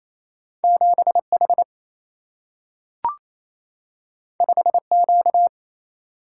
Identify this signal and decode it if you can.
Morse code, heard as a keyed tone, about 28 wpm, 700 Hz. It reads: 7 H 5 Q
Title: Single-Letter Single-Number 28wpm